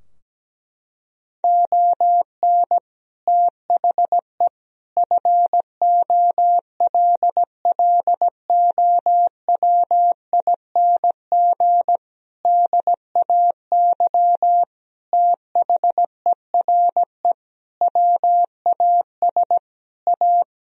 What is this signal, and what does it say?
Morse code, heard as a keyed tone, about 17 wpm, 700 Hz. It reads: ON THE FOLLOWING DAY THERE WAS A
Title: Sentences from Top 1000 Words - Review Part 4 - 17wpm